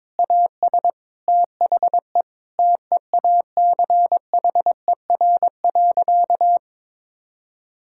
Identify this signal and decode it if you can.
Morse code, heard as a keyed tone, about 22 wpm, 700 Hz. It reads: AS THE TEACHER.
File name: practice / 22wpm / Sentences with 4-Word Phrases - 22wpm